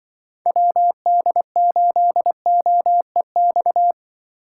Morse code, heard as a keyed tone, about 24 wpm, 700 Hz. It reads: WD8OEX
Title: Speed Racing - USA Call Signs 20wpm